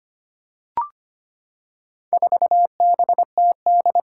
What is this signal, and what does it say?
Morse code, heard as a keyed tone, about 25 wpm, 700 Hz. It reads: 4BTD
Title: Four Challenging Letter-Number Combinations 25wpm